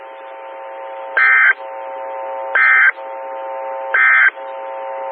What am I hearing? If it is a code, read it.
Emergency Alert System (SAME) tones, heard as harsh buzzing bursts. It is heard from 1.1 to 4.4 s.
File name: WX9MCS 147255 MHz SKYWARN Repeater